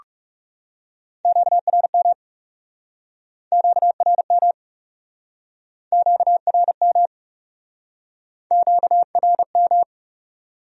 Morse code, heard as a keyed tone, about 37 wpm, 700 Hz. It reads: QRM QRM QRM QRM
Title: Speed-Racing Q-Codes - 30wpm